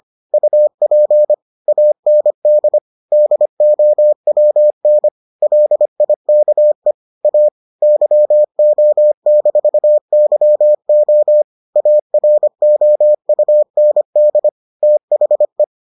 Morse code, heard as a keyed tone, about 25 wpm, 600 Hz. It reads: UP AND DOWN LIKE A YO-YO AROUND THE